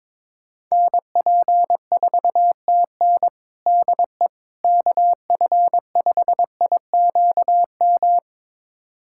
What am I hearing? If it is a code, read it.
Morse code, heard as a keyed tone, about 21 wpm, 700 Hz. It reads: NP4TN DE KF5IQM
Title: USA Callsign DE Callsign 22wpm